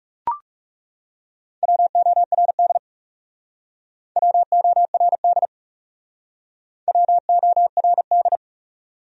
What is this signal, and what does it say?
Morse code, heard as a keyed tone, about 37 wpm, 700 Hz. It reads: WORD WORD WORD
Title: Speed-Racing Top 100 Words 30wpm